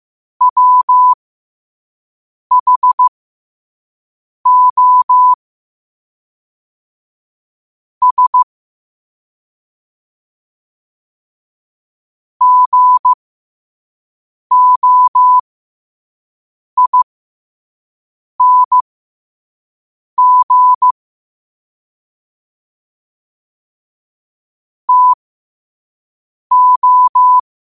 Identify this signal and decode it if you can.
Morse code, heard as a keyed tone, about 15 wpm, spaced as for 5 wpm, 1000 Hz. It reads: WHO S GOING TO
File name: qotdfiles / 5wpm